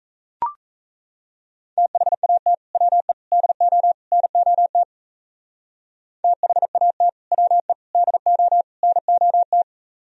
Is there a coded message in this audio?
Yes: Morse code from 1.8 to 9.6 s, heard as a keyed tone, about 35 wpm, 700 Hz. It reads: THAT WE DO NOT THAT WE DO NOT